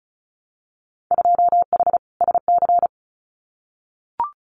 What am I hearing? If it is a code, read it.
Morse code, heard as a keyed tone, about 36 wpm, 690 Hz. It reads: 2H SC
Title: Winter Field Day Exchanges 35wpm